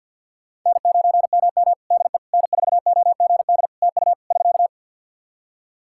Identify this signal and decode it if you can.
Morse code, heard as a keyed tone, about 52 wpm, 700 Hz. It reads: N9MM DE N4OGD TU <SK>
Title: K1USN SST QSO Practice 50wpm